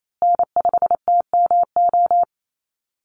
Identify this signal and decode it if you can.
Morse code, heard as a keyed tone, about 28 wpm, 700 Hz. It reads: N5TMO
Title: USA Call Signs Repeated 3x 28wpm